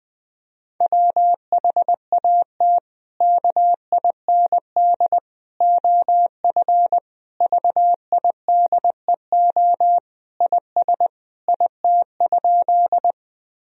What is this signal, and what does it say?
Morse code, heard as a keyed tone, about 21 wpm, 700 Hz. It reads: WHAT KIND OF VIDEO IS IT?
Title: Sentences from Top 1000 Words - Review Part 4 - 20wpm